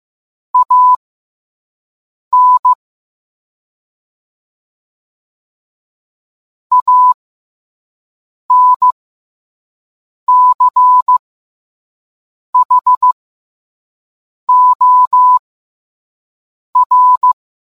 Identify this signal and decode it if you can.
Morse code, heard as a keyed tone, about 15 wpm, spaced as for 5 wpm, 1000 Hz. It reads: AN ANCHOR